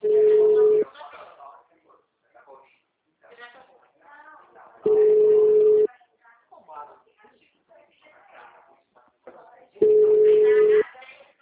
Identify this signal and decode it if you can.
Ringback tone (Europe, 425 Hz), with speech.